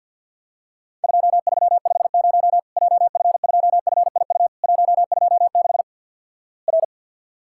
Morse code, heard as a keyed tone, about 50 wpm, 700 Hz. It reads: 2350 JF2FIU 126 R
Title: Worked All Europe DX Contest - QTC Exchange with Repeats - 50wpm